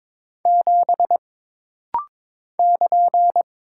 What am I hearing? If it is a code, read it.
Morse code, heard as a keyed tone, about 22 wpm, 700 Hz. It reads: 7 <KN>
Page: This rapid-fire timing is geared to push your ICR (Instant Character Recognition) skills.